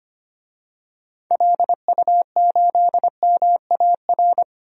Morse code, heard as a keyed tone, about 25 wpm, 700 Hz. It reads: LU8MAR
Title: DXCC-100 Call Signs - 6 Characters - Repeated 2x - 25wpm